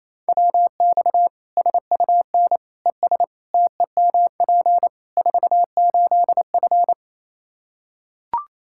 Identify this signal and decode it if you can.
Morse code, heard as a keyed tone, about 27 wpm, 700 Hz. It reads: WX SUN ES TEMP 48F
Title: K1USN SST QSO Practice 28wpm